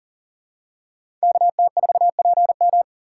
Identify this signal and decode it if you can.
Morse code, heard as a keyed tone, about 40 wpm, 700 Hz. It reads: KT4PM
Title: USA Call Signs - 4x Character Spacing - 40wpm